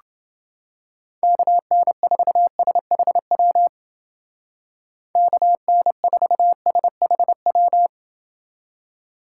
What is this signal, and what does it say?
Morse code, heard as a keyed tone, about 30 wpm, 700 Hz. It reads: KN4SHW KN4SHW